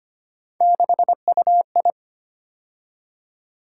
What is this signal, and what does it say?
Morse code, heard as a keyed tone, about 25 wpm, 700 Hz. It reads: 6UI
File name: Three Letter-Number Combinations - 25wpm